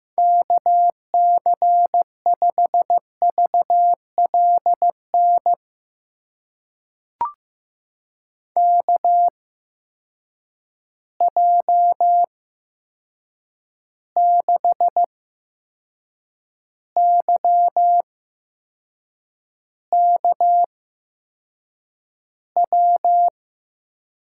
Morse code, heard as a keyed tone, about 15 wpm, spaced as for 6 wpm, 700 Hz. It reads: KC5VLN KJ6YKW
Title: USA Call Signs - 6 Characters - 8x Character Spacing - 15wpm